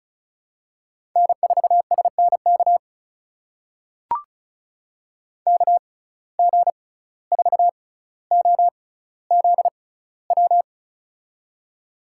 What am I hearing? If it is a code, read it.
Morse code, heard as a keyed tone, about 34 wpm, spaced as for 17 wpm, 700 Hz. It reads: N4SNK KG4OZW
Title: USA Call Signs - 6x Character Spacing - 35wpm